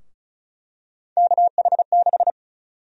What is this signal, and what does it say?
Morse code, heard as a keyed tone, about 36 wpm, 700 Hz. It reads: KH6
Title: USA Call Sign Prefixes - Repeated 2x - 35wpm